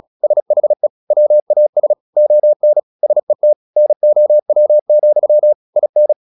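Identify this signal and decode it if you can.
Morse code, heard as a keyed tone, about 36 wpm, 600 Hz. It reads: SHE WAS ON SET NOW, IN
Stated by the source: Morse practice at 36 wpm, text of Ken Follett's novel Never, from CW med Gnister.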